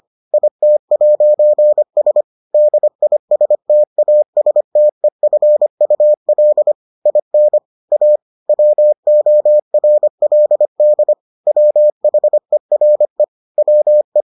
Morse code, heard as a keyed tone, about 25 wpm, 600 Hz. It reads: IT'S DISTASTEFUL IN A WORLD WHERE WE